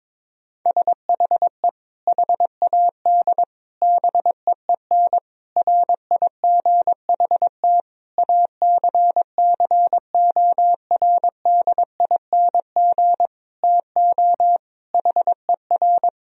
Morse code, heard as a keyed tone, about 22 wpm, 700 Hz. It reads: SHE HAD BEEN RIGHT ACCORDING TO HER
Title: Sentences from Top 1000 Words - Review Part 4 - 22wpm